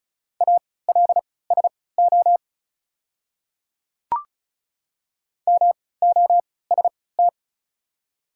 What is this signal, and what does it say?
Morse code, heard as a keyed tone, about 36 wpm, spaced as for 20 wpm, 700 Hz. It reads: ALSO MOST
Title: 3x Character Spacing - Top 200 Words - 35wpm